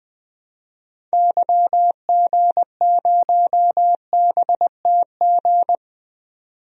Morse code, heard as a keyed tone, about 20 wpm, 700 Hz. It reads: YG0BTG
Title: DXCC-100 Call Signs - 6 Characters - 20wpm